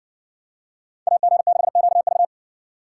Morse code, heard as a keyed tone, about 59 wpm, 700 Hz. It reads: AG6QV